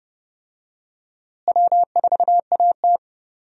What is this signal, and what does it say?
Morse code, heard as a keyed tone, about 30 wpm, 700 Hz. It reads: W4AT
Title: USA Call Signs - 4 Characters - Repeated 2x - 30wpm